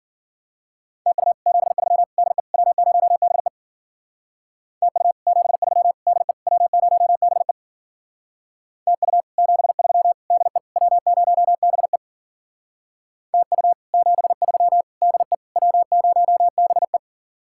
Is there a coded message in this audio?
Yes: Morse code from 1.1 to 17.0 s, heard as a keyed tone, about 51 wpm, 700 Hz. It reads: TU 73 DE W0BE TU 73 DE W0BE TU 73 DE W0BE TU 73 DE W0BE